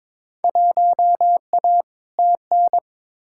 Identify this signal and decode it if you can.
Morse code, heard as a keyed tone, about 22 wpm, 700 Hz. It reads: 1A TN
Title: Field Day Exchanges - 22wpm